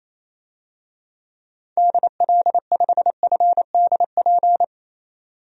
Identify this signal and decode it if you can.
Morse code, heard as a keyed tone, about 28 wpm, 700 Hz. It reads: DL5FDP
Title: DXCC-100 Call Signs - 6 Characters - 28wpm